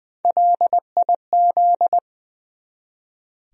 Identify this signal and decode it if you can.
Morse code, heard as a keyed tone, about 20 wpm, 700 Hz. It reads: LIZ
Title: USA Common Names - 20wpm